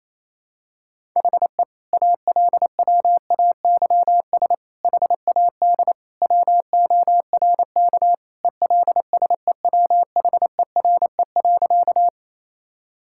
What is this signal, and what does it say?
Morse code, heard as a keyed tone, about 27 wpm, 700 Hz. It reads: HE ALWAYS HAD WORK ELSEWHERE.
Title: Sentences from Top 1600 Words 28wpm